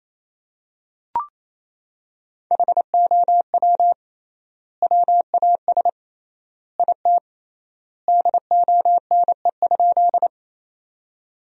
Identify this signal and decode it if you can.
Morse code, heard as a keyed tone, about 27 wpm, 700 Hz. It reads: HOW WAS IT DONE?
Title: 3x Word-Spacing - Sentences from Top 500 Words - 28wpm